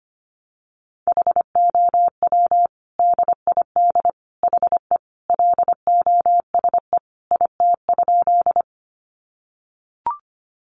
Morse code, heard as a keyed tone, about 25 wpm, 700 Hz. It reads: HOW DID HE LOSE IT?
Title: Sentences from Top 1100 Words 25wpm